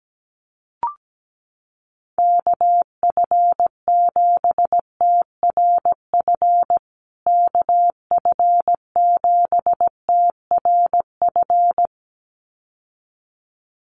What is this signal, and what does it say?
Morse code, heard as a keyed tone, about 17 wpm, 700 Hz. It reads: KF7TRF KF7TRF
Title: USA Call Signs Repeated 2x 17wpm